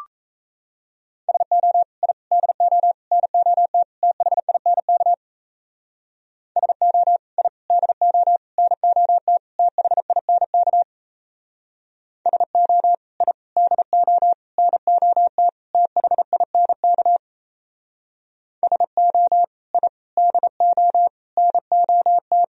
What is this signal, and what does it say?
Morse code, heard as a keyed tone, about 36 wpm, 700 Hz. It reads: SO I DO NOT THINK SO I DO NOT THINK SO I DO NOT THINK SO I DO NOT